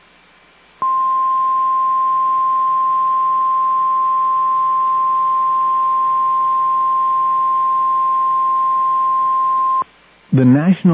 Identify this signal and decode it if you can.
Emergency Alert System (SAME) tones (the 1050 Hz attention tone), with speech. It is heard from 0.8 to 9.9 s.